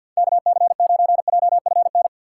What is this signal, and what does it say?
Morse code, heard as a keyed tone, about 50 wpm, 700 Hz. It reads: KC9JFN
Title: USA Call Signs - 6 Characters - 8x Character Spacing - 50wpm